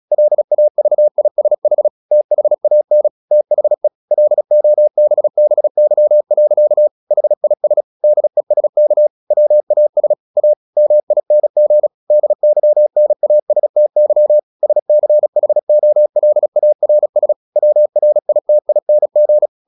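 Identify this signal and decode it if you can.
Morse code, heard as a keyed tone, about 36 wpm, 600 Hz. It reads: LAVISH THAN THE LOBBY. HIS DESK WAS A MING DYNASTY SCHOLARS WRITING
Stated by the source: Morse practice at 36 wpm, text of Ken Follett's novel Never, from CW med Gnister.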